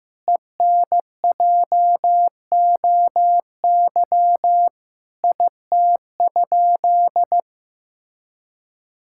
Morse code, heard as a keyed tone, about 16 wpm, 700 Hz. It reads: ENJOY IT?
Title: Sentences from Top 1000 Words - Review Part 5 - 15wpm